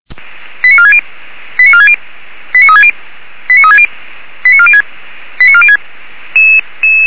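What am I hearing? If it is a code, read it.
Five-tone selective call (ZVEI), heard as quick runs of five tones, tones of about 70 ms. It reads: 89460 89460 89360 89360 89406 89406
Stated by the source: ZVEI five-tone call received by SDR.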